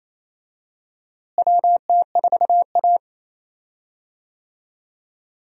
Morse code, heard as a keyed tone, about 28 wpm, 700 Hz. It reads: WT4A
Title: USA Call Signs - 4 Characters - Repeated 3x - 28wpm